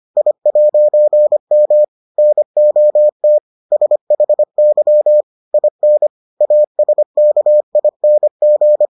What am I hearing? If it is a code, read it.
Morse code, heard as a keyed tone, about 25 wpm, 600 Hz. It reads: I'M NOT SHY IN ASKING